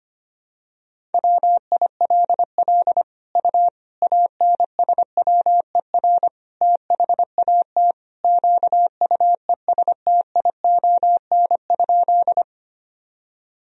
Morse code, heard as a keyed tone, about 25 wpm, 700 Hz. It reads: WILL U ANSWER THAT QUESTION?
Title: Sentences from Top 400 Words and Abbreviations 25wpm